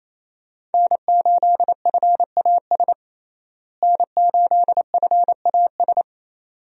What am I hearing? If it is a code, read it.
Morse code, heard as a keyed tone, about 28 wpm, 700 Hz. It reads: N8FAS N8FAS
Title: USA Call Signs - 5 Characters - Repeated 2x - 28wpm